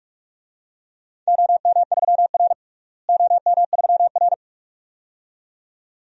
Morse code, heard as a keyed tone, about 46 wpm, 700 Hz. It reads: OM3R OM3R
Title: DXCC-100 Call Signs - 4 Characters - Repeated 2x - 45wpm